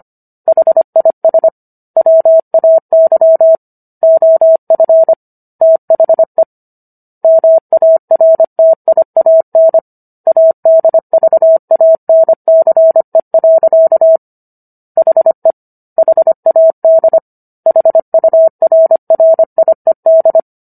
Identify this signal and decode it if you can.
Morse code, heard as a keyed tone, about 25 wpm, 650 Hz. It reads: HIS WAY OF THE MARTIAN ADVANCE. HE HAD HURRIED